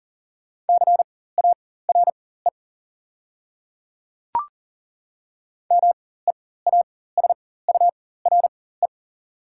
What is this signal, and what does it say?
Morse code, heard as a keyed tone, about 40 wpm, spaced as for 19 wpm, 690 Hz. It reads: CARE MEASURE
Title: 4x Character Spacing - Top 500 Words - 40wpm